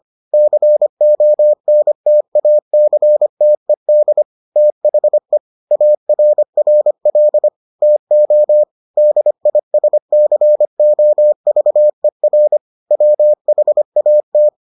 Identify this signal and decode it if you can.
Morse code, heard as a keyed tone, about 25 wpm, 600 Hz. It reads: CONTACTED THE ARRL TO DISCOVER WHAT